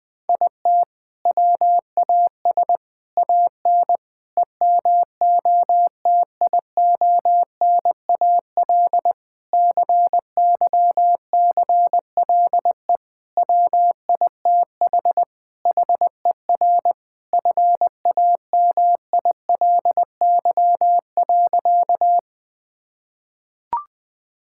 Morse code, heard as a keyed tone, about 21 wpm, 700 Hz. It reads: IT WAS AN EMOTIONAL CYCLE WITH HER FAMILY.
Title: Sentences from Top 1000 Words - Review Part 6 - 20wpm